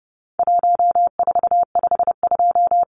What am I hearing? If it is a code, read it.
Morse code, heard as a keyed tone, about 30 wpm, 700 Hz. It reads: 1452
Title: Numbers - Contest Style - 4-Digit 30wpm